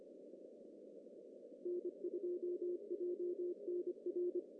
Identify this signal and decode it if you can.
Morse code, heard as a keyed tone, about 25 wpm, 350 Hz. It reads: N2JNR
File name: CW / cqww-cw-k5tr